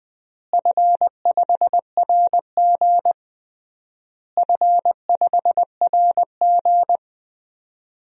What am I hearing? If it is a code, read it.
Morse code, heard as a keyed tone, about 20 wpm, 700 Hz. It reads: F5RG F5RG
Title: DXCC-100 Call Signs - 4 Characters - Repeated 3x - 20wpm